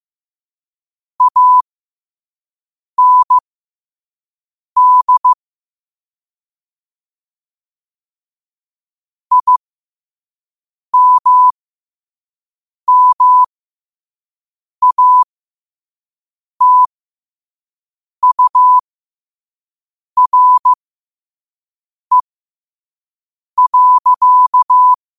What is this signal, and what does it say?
Morse code, heard as a keyed tone, about 15 wpm, spaced as for 5 wpm, 1000 Hz. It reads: AND IMMATURE.
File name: qotdfiles / 5wpm